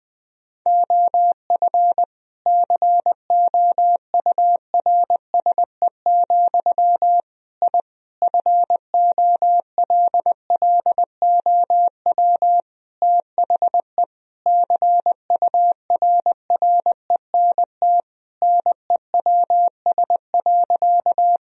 Morse code, heard as a keyed tone, about 21 wpm, 700 Hz. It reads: OF COURSE, I FOLLOW THE CURRENT NEWS.
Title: Sentences from Top 1000 Words - Review Part 4 - 20wpm